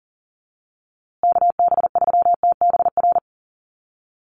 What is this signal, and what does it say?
Morse code, heard as a keyed tone, about 40 wpm, 700 Hz. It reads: KB3TBR
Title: USA Call Signs Repeated 3x 40wpm